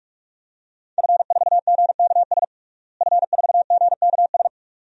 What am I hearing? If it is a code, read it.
Morse code, heard as a keyed tone, about 41 wpm, 700 Hz. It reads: F4GKS F4GKS
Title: DXCC-100 Call Signs - 5 Characters - Repeated 2x - 45wpm